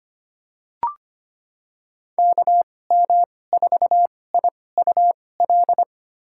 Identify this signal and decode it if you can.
Morse code, heard as a keyed tone, about 25 wpm, spaced as for 18 wpm, 700 Hz. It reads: KM4IUL
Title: USA Call Signs - 2x Character Spacing - 25wpm